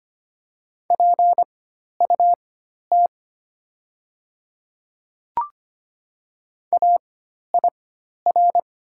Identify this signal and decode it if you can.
Morse code, heard as a keyed tone, about 25 wpm, spaced as for 12 wpm, 690 Hz. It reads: PUT AIR
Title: 4x Character Spacing - Top 200 Words - 25wpm